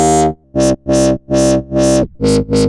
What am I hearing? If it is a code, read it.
There is no code.